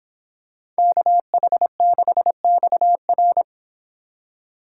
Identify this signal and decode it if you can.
Morse code, heard as a keyed tone, about 26 wpm, 700 Hz. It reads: KH6XR